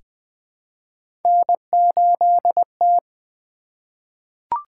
Morse code, heard as a keyed tone, about 20 wpm, 700 Hz. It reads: N8T
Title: Three Letter-Number Combinations - 20wpm